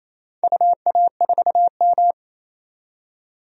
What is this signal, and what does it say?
Morse code, heard as a keyed tone, about 28 wpm, 700 Hz. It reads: UA4M
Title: DXCC-100 Call Signs - 4 Characters - 28wpm